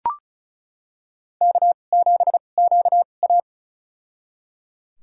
Morse code, heard as a keyed tone, about 34 wpm, spaced as for 25 wpm, 700 Hz. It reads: K7QA
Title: USA Call Signs - 4 Characters - 2x Character Spacing - 35wpm